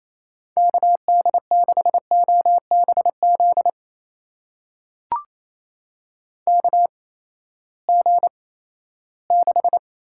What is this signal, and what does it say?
Morse code, heard as a keyed tone, about 28 wpm, spaced as for 16 wpm, 700 Hz. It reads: KD6OBZ KG6
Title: USA Call Signs - 6 Characters - 8x Character Spacing - 28wpm